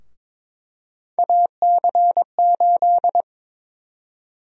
Morse code, heard as a keyed tone, about 22 wpm, 700 Hz. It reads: AC8
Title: USA Call Sign Prefixes - 22wpm